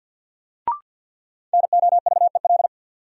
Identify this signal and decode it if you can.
Morse code, heard as a keyed tone, about 47 wpm, 700 Hz. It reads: NOVEL
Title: Sentences from Top 1300 Words - New Word Drill - 50wpm